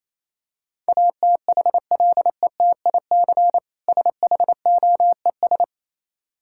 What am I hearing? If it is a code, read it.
Morse code, heard as a keyed tone, about 27 wpm, 700 Hz. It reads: ATHLETIC SHOES
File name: Easy Words - Speed-Racing with Hints - 22wpm